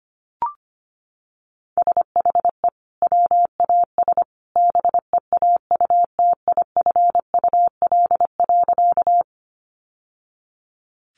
Morse code, heard as a keyed tone, about 25 wpm, 700 Hz. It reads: SHE WAS BEAUTIFUL.
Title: Sentences from Top 900 Words 25wpm